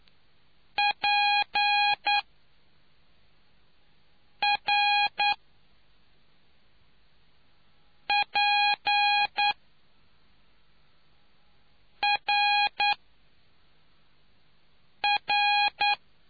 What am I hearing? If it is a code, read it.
Morse code, heard as a keyed tone, about 9 wpm, spaced as for 3 wpm, 820 Hz. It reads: PRPRR